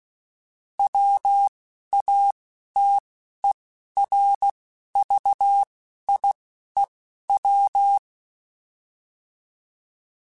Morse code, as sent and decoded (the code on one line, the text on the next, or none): .-- .- - . .-. ...- .. .--
WATERVIEW